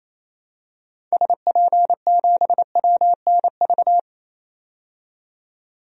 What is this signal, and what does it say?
Morse code, heard as a keyed tone, about 27 wpm, 700 Hz. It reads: SP7WNV
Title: DXCC-100 Call Signs - 6 Characters - 28wpm